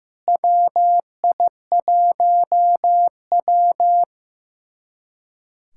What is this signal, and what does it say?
Morse code, heard as a keyed tone, about 15 wpm, 700 Hz. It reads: WI1W
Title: USA Callsign DE Callsign - 4 Characters - 15wpm